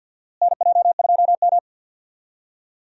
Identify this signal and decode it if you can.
Morse code, heard as a keyed tone, about 51 wpm, 700 Hz. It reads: NJ2M
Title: USA Call Signs - 4 Characters - Repeated 2x - 50wpm